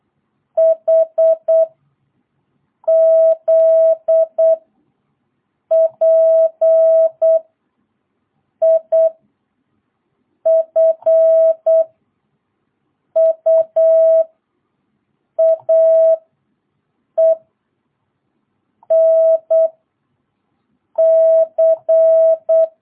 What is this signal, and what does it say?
Morse code, heard as a keyed tone, about 8 wpm, 660 Hz. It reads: H Z P I F U A E N C
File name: jh8cbh / 2022cw